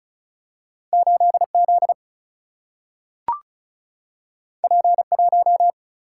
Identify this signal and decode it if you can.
Morse code, heard as a keyed tone, about 36 wpm, 700 Hz. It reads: 8 Z P 1
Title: Single-Letter Single-Number 35wpm